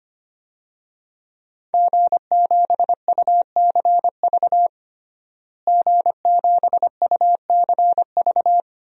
Morse code, heard as a keyed tone, about 25 wpm, 700 Hz. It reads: G7UCV G7UCV
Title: DXCC-100 Call Signs - 5 Characters - Repeated 2x - 25wpm